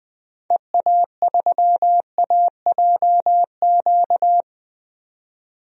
Morse code, heard as a keyed tone, about 20 wpm, 700 Hz. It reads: EA3AJQ
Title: DXCC-100 Call Signs - 6 Characters - 20wpm